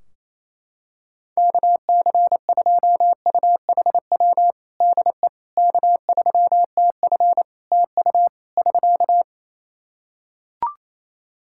Morse code, heard as a keyed tone, about 28 wpm, 700 Hz. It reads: KC2UHW DE K3TF TU <SK>
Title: Practice QSO - Exchange 4 - Ending - 28wpm